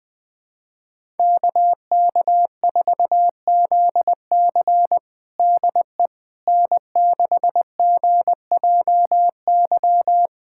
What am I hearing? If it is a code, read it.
Morse code, heard as a keyed tone, about 20 wpm, 700 Hz. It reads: KK4ZC DE N6GJY
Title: USA Callsign DE Callsign - 5 Characters - 20wpm